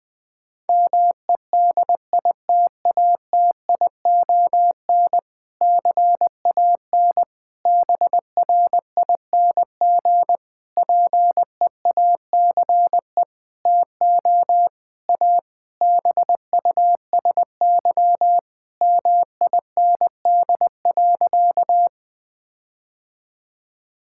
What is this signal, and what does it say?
Morse code, heard as a keyed tone, about 20 wpm, 700 Hz. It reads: MEDITATION CAN BRING PEACE TO A BUSY MIND.
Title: Difficult Sentences with Hints - 20wpm